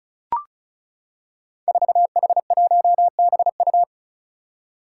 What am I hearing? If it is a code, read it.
Morse code, heard as a keyed tone, about 36 wpm, 700 Hz. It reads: 4H1BU
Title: Five Challenging Letter-Number Combinations 35wpm